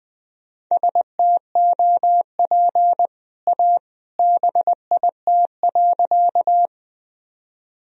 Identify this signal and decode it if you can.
Morse code, heard as a keyed tone, about 20 wpm, 700 Hz. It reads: STOP A BIT.
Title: Sentences from Top 1000 Words - Review Part 1 - 20wpm